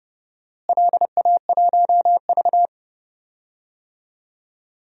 Morse code, heard as a keyed tone, about 30 wpm, 700 Hz. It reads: LA1V
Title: DXCC-100 Call Signs - 4 Characters - Repeated 2x - 30wpm